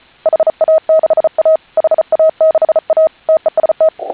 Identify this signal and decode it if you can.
Morse code, heard as a keyed tone, about 34 wpm, 630 Hz. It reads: HA6A HA6A TEST
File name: HA6A CW